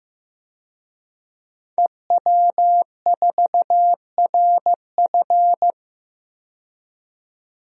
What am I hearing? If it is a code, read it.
Morse code, heard as a keyed tone, about 15 wpm, 700 Hz. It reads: EW4RF
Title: DXCC-100 Call Signs - 5 Characters - 15wpm